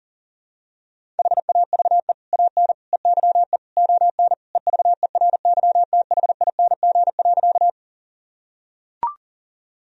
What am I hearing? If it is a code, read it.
Morse code, heard as a keyed tone, about 37 wpm, 700 Hz. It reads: HAVE AN EYE ON EVERYTHING.
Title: Sentences from Top 1000 Words - Review Part 1 - 40wpm